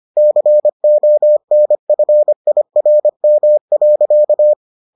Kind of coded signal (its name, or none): Morse code